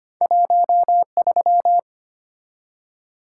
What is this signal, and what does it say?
Morse code, heard as a keyed tone, about 25 wpm, 700 Hz. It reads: 1 3
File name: Two Letter-Number Combinations - 25wpm